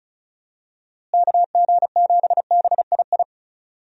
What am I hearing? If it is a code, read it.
Morse code, heard as a keyed tone, about 36 wpm, 700 Hz. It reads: KG7BII
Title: USA Call Signs - 6 Characters - Repeated 2x - 35wpm